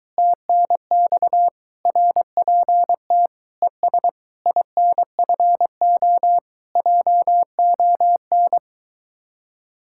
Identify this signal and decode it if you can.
Morse code, heard as a keyed tone, about 24 wpm, 700 Hz. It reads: TNX RPT ES INFO JON